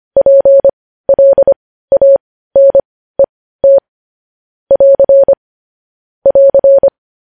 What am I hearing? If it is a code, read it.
Morse code, heard as a keyed tone, about 25 wpm, spaced as for 15 wpm, 550 Hz. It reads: PLANET <AR> <AR>
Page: news headlines in morse code at 15 wpm